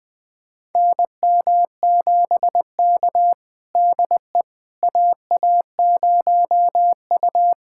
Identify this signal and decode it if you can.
Morse code, heard as a keyed tone, about 20 wpm, 700 Hz. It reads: NM7K DE AA0U
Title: USA Callsign DE Callsign - 4 Characters - 20wpm